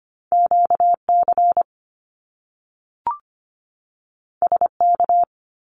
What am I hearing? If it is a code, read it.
Morse code, heard as a keyed tone, about 25 wpm, 700 Hz. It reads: QC SK
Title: Canadian Provinces and Territories 25wpm